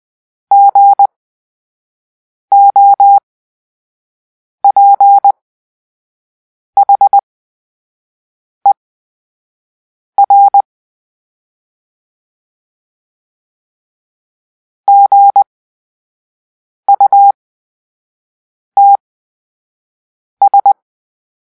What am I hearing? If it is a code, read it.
Morse code, heard as a keyed tone, about 20 wpm, spaced as for 5 wpm, 800 Hz. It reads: GOPHER GUTS
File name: Grp-3 Drill-4